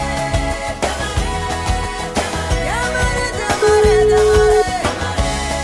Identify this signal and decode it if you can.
Ringback tone (Europe, 425 Hz), with noise.